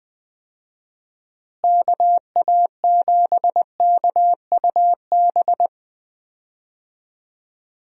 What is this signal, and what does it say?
Morse code, heard as a keyed tone, about 20 wpm, 700 Hz. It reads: KA7KUB
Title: USA Call Signs - 6 Characters - Repeated 2x - 20wpm